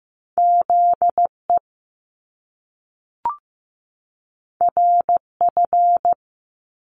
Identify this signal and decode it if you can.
Morse code, heard as a keyed tone, about 15 wpm, 700 Hz. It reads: ZE RF
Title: Top Two Letter Combinations in Words 15wpm